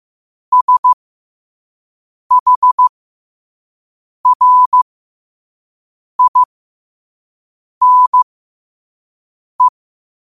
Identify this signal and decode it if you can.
Morse code, heard as a keyed tone, about 15 wpm, spaced as for 5 wpm, 1000 Hz. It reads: SHRINE
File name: qotdfiles / 5wpm